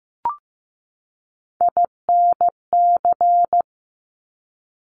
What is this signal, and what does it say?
Morse code, heard as a keyed tone, about 15 wpm, 700 Hz. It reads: INC
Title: Top Three Letter Combinations in Words 15wpm